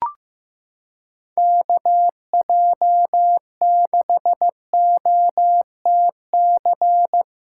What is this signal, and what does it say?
Morse code, heard as a keyed tone, about 16 wpm, 700 Hz. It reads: KJ6OTC
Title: USA Callsign DE Callsign 15wpm